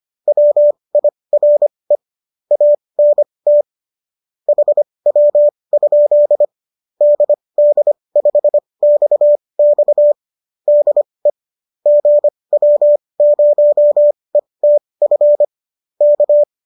Morse code, heard as a keyed tone, about 25 wpm, spaced as for 20 wpm, 590 Hz. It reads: WIRE ANT HW? DD5XX DE GW0ETF K